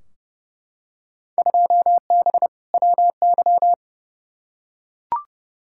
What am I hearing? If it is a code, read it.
Morse code, heard as a keyed tone, about 32 wpm, 700 Hz. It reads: 2B WY
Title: Field Day Exchanges Encore - 30wpm